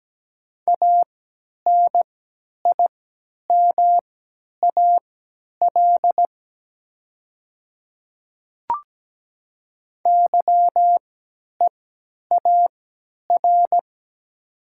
Morse code, heard as a keyed tone, about 17 wpm, spaced as for 10 wpm, 700 Hz. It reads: ANIMAL YEAR
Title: 3x Character Spacing - Top 300 Words - 17wpm